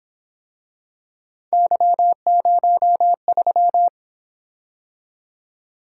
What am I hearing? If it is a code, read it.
Morse code, heard as a keyed tone, about 28 wpm, 700 Hz. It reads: Y03